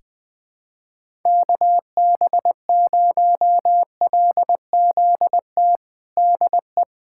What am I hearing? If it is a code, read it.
Morse code, heard as a keyed tone, about 20 wpm, 700 Hz. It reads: KB0LZT DE
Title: USA Callsign DE Callsign - 6 Characters - 20wpm